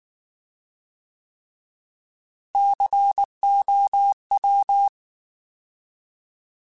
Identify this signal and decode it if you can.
Morse code, heard as a keyed tone, about 19 wpm, 800 Hz. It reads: COW
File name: Alphabetical / Alphabetical mp3s / Cs @18